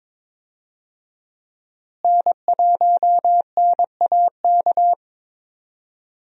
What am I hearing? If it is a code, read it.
Morse code, heard as a keyed tone, about 22 wpm, 700 Hz. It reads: N1NAK
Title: USA Call Signs - 5 Characters - Repeated 2x - 22wpm